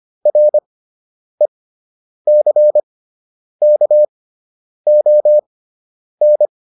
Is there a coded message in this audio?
Yes: Morse code from 0.3 to 6.5 s, heard as a keyed tone, about 25 wpm, spaced as for 9 wpm, 600 Hz. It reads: RECKON